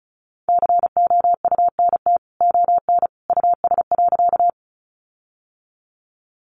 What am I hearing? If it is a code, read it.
Morse code, heard as a keyed tone, about 34 wpm, 700 Hz. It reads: COUNT ON US.
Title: Sentences from Top 1500 Words 35wpm